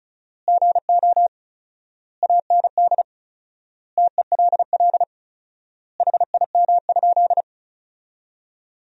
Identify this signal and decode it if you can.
Morse code, heard as a keyed tone, about 36 wpm, 700 Hz. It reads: GO AND TELL HIM?
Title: 4x Word-Spacing - Sentences from Top 200 Words - 35wpm